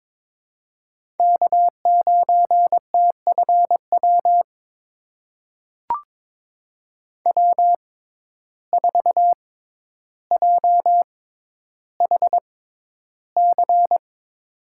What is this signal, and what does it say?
Morse code, heard as a keyed tone, about 22 wpm, spaced as for 11 wpm, 700 Hz. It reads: K9TFW W4JHC
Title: USA Call Signs - 6x Character Spacing - 22wpm